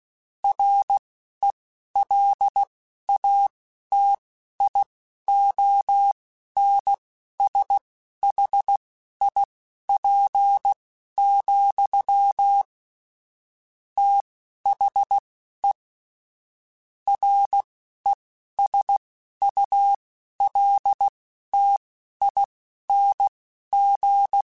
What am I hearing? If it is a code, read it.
Morse code, heard as a keyed tone, about 16 wpm, spaced as for 11 wpm, 800 Hz. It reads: RELATIONSHIP, THE RESULTING
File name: Technical Subjects / Technical Subjects-mp3s / XC @10